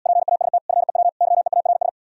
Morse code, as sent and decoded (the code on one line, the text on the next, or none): .--. .- ... - / ..-. .-- / -.-- . .- .-. ...
PAST FEW YEARS